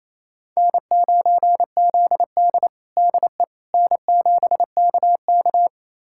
Morse code, heard as a keyed tone, about 28 wpm, 700 Hz. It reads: N9ZD DE N7KK
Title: USA Callsign DE Callsign - 4 Characters - 28wpm